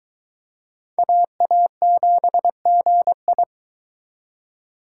Morse code, heard as a keyed tone, about 24 wpm, 700 Hz. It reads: AA7GI